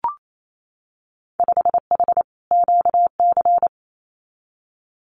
Morse code, heard as a keyed tone, about 28 wpm, 700 Hz. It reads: 5H QC
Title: Winter Field Day Exchanges Encore 28wpm